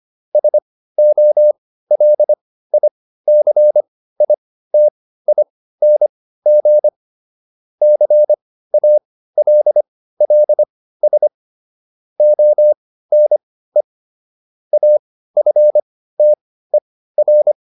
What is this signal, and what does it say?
Morse code, heard as a keyed tone, about 25 wpm, spaced as for 15 wpm, 600 Hz. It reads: SOLICITING CALLS ONE AFTER